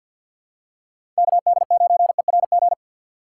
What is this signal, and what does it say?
Morse code, heard as a keyed tone, about 52 wpm, 700 Hz. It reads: KD9ERG
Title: USA Call Signs - 6 Characters - 3x Character Spacing - 50wpm